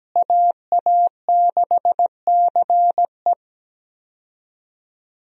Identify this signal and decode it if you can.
Morse code, heard as a keyed tone, about 17 wpm, 700 Hz. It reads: AA6CE
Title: USA Call Signs - 5 Characters - Repeated 3x - 17wpm